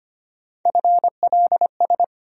Morse code, heard as a keyed tone, about 25 wpm, 700 Hz. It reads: FLS
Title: USA Call Sign Postfixes - 3 Letters 25wpm